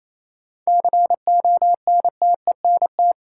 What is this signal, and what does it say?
Morse code, heard as a keyed tone, about 28 wpm, 700 Hz. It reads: CONTENT
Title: Sentences from Top 900 Words - New Word Drill - 28wpm